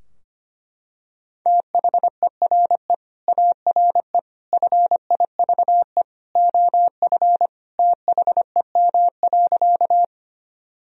Morse code, heard as a keyed tone, about 25 wpm, 700 Hz. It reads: THERE ARE FIVE OF THEM.
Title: Sentences from Top 1000 Words - Review Part 2 - 25wpm